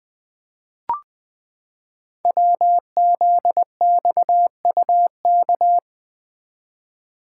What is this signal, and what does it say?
Morse code, heard as a keyed tone, about 20 wpm, 700 Hz. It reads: WZXUK
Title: Code Groups - 5 Letter:Numbers - 20wpm